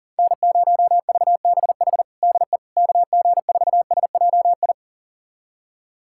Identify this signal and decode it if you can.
Morse code, heard as a keyed tone, about 40 wpm, 700 Hz. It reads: N0VBH DE KG4SJI
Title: USA Callsign DE Callsign 40wpm